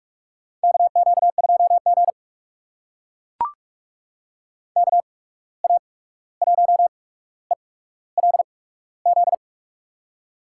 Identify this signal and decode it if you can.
Morse code, heard as a keyed tone, about 44 wpm, spaced as for 17 wpm, 700 Hz. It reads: KQ2G KA1ELZ
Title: USA Call Signs - 8x Character Spacing - 45wpm